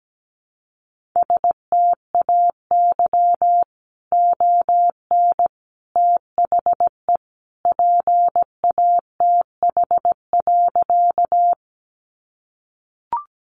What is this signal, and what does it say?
Morse code, heard as a keyed tone, about 17 wpm, 700 Hz. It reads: STAY ON THE PATH.
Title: A Sound Of Thunder 17wpm